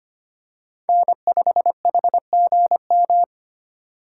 Morse code, heard as a keyed tone, about 25 wpm, 700 Hz. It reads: N5HGM
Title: USA Call Signs - 5 Characters - 4x Character Spacing - 25wpm